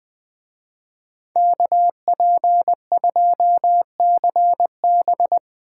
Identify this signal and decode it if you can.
Morse code, heard as a keyed tone, about 22 wpm, 700 Hz. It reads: KP2CB
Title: Speed Racing - USA 2x2 Call Signs 20wpm